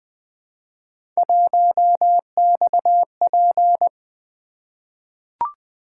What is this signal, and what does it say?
Morse code, heard as a keyed tone, about 20 wpm, 700 Hz. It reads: 1XP
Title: Three Letter-Number Combinations - 20wpm